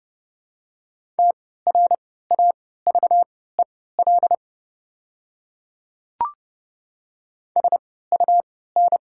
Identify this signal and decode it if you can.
Morse code, heard as a keyed tone, about 30 wpm, spaced as for 17 wpm, 700 Hz. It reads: TRAVEL SUN
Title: 3x Character Spacing - Top 500 Words - 30wpm